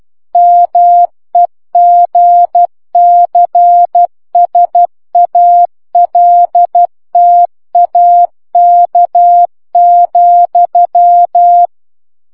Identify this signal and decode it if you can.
Morse code, heard as a keyed tone, about 12 wpm, 700 Hz. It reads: MEGCSALTAK,